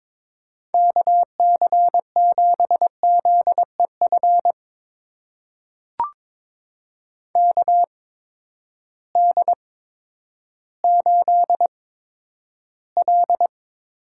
Morse code, heard as a keyed tone, about 22 wpm, spaced as for 11 wpm, 700 Hz. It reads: KC7ZEF KD8L
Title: USA Call Signs - 6 Characters - 8x Character Spacing - 22wpm